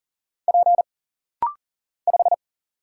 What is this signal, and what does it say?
Morse code, heard as a keyed tone, about 39 wpm, 700 Hz. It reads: P 5
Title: Single Letter-Number - Rapid-Fire 40wpm